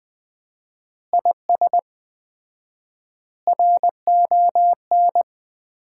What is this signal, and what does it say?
Morse code, heard as a keyed tone, about 20 wpm, 700 Hz. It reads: IS RON
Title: Common QSO Phrases - 4x Word-Spacing - 20wpm